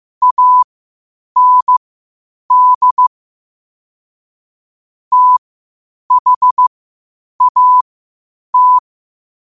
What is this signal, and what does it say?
Morse code, heard as a keyed tone, about 15 wpm, spaced as for 8 wpm, 1000 Hz. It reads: AND THAT
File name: qotdfiles / 8wpm